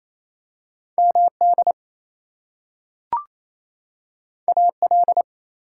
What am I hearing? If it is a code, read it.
Morse code, heard as a keyed tone, about 28 wpm, 700 Hz. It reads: MD AL
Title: US States and Canadian Provinces 28wpm